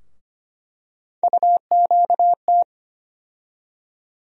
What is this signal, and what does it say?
Morse code, heard as a keyed tone, about 25 wpm, 700 Hz. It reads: UQT
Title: USA Call Sign Postfixes - 3 Letters 25wpm